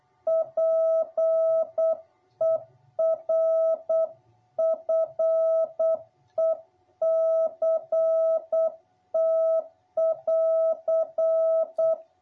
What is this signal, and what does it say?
Morse code, heard as a keyed tone, about 8 wpm, 660 Hz. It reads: PERFECT<AR>
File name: jh8cbh / 2021cw